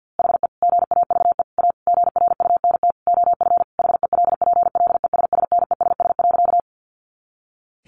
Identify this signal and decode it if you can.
Morse code, heard as a keyed tone, about 50 wpm, 700 Hz. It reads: HE GAVE A GRUNT OF HELPLESSNESS.
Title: A Sound Of Thunder 50wpm